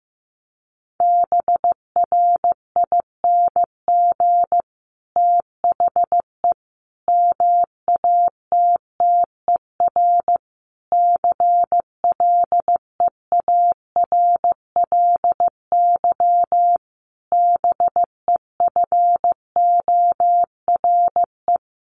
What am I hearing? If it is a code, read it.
Morse code, heard as a keyed tone, about 15 wpm, 700 Hz. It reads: BRING THE MATTER CLEARLY BEFORE
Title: Sentences from Top 1300 Words 15wpm